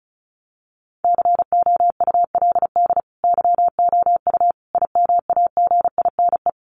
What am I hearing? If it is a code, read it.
Morse code, heard as a keyed tone, about 34 wpm, 700 Hz. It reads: COULD YOU IMAGINE
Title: Sentences from Top 1400 Words 35wpm